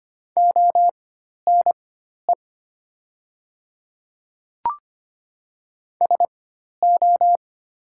Morse code, heard as a keyed tone, about 25 wpm, spaced as for 12 wpm, 700 Hz. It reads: ONE SO
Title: 4x Character Spacing - Top 100 Words - 25wpm